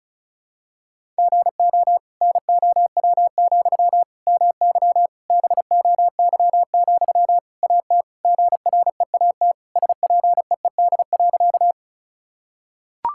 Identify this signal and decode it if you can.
Morse code, heard as a keyed tone, about 36 wpm, 700 Hz. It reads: GO NOW, MY BOY, AT GREAT SPEED.
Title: Sentences from Top 1000 Words - Review Part 5 - 35wpm